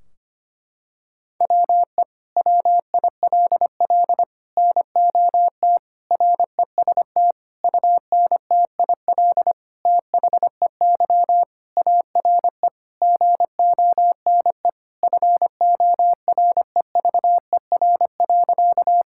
Morse code, heard as a keyed tone, about 25 wpm, 700 Hz. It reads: WE WILL NOT REST UNTIL THEY ARE GONE FOREVER.
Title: Sentences from Top 1600 Words 25wpm